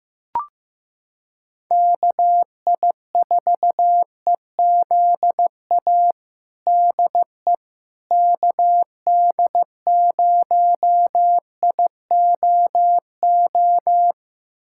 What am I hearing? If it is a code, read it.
Morse code, heard as a keyed tone, about 15 wpm, 700 Hz. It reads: KI4EZA DE KD0IOO
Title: USA Callsign DE Callsign - 6 Characters - 15wpm